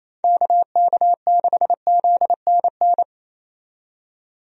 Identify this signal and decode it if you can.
Morse code, heard as a keyed tone, about 28 wpm, 700 Hz. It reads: KK6ZNN
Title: USA Call Signs - 6 Characters - 28wpm